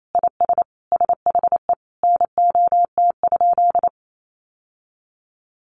Morse code, heard as a keyed tone, about 28 wpm, 700 Hz. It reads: IS SHE NOT?
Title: Sentences from Top 1300 Words 28wpm